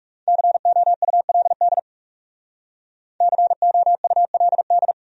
Morse code, heard as a keyed tone, about 40 wpm, 700 Hz. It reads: COULD COULD